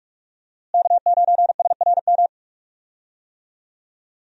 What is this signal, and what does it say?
Morse code, heard as a keyed tone, about 46 wpm, 700 Hz. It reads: K9SRM
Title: USA Call Signs - 5 Characters - Repeated 2x - 45wpm